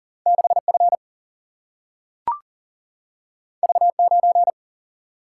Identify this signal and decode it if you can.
Morse code, heard as a keyed tone, about 40 wpm, 700 Hz. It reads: 6 F V 9
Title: Single-Letter Single-Number 40wpm